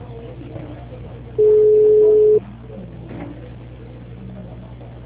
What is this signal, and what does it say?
Ringback tone (Europe, 425 Hz), with speech.
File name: out- -2002- - - .0